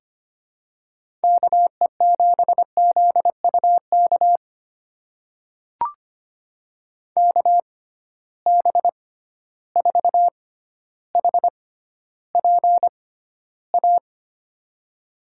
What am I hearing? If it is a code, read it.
Morse code, heard as a keyed tone, about 25 wpm, spaced as for 13 wpm, 700 Hz. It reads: KE7ZUK KB4HPA
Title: USA Call Signs - 6 Characters - 6x Character Spacing - 25wpm